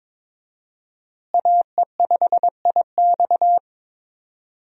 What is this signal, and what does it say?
Morse code, heard as a keyed tone, about 22 wpm, 700 Hz. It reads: AE5IX
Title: USA Call Signs - 5 Characters - Repeated 3x - 22wpm